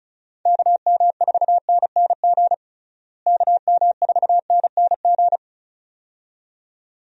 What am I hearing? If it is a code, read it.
Morse code, heard as a keyed tone, about 34 wpm, 700 Hz. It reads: KM4NNG KM4NNG
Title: USA Call Signs - 6 Characters - Repeated 2x - 35wpm